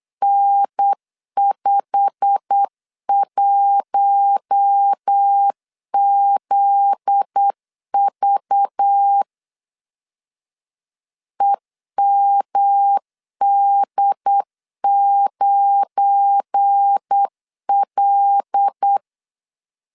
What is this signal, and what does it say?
Morse code, heard as a keyed tone, about 8 wpm, 800 Hz. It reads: N51ZV EMD9L
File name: LEZIONI / 8 YJ45 / 8 YJ45 8wpm 3214